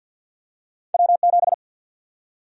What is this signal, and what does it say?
Morse code, heard as a keyed tone, about 48 wpm, 700 Hz. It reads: W7
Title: USA Call Sign Prefixes - 50wpm